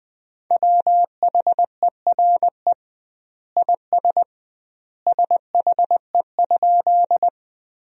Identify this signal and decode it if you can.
Morse code, heard as a keyed tone, about 21 wpm, 700 Hz. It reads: WHERE IS SHE?
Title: 2x Word-Spacing - Sentences from Top 200 Words - 20wpm